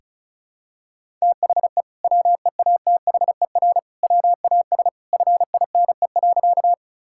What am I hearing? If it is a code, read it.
Morse code, heard as a keyed tone, about 36 wpm, 700 Hz. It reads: THE WEATHER WAS FINE.
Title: Sentences from Top 1000 Words - Review Part 5 - 35wpm